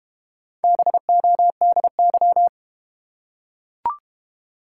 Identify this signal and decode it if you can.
Morse code, heard as a keyed tone, about 30 wpm, 700 Hz. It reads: BODY
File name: Easy Words - Speed-Racing with Hints - 25wpm